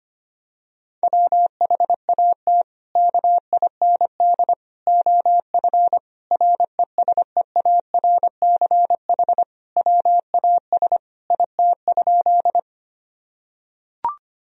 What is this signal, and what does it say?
Morse code, heard as a keyed tone, about 25 wpm, 700 Hz. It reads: WHAT KIND OF RESEARCH WAS IT?
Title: Sentences from Top 1000 Words - Review Part 4 - 25wpm